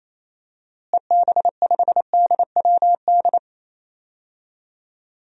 Morse code, heard as a keyed tone, about 28 wpm, 700 Hz. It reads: EB5DWD
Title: DXCC-100 Call Signs - 6 Characters - Repeated 3x - 28wpm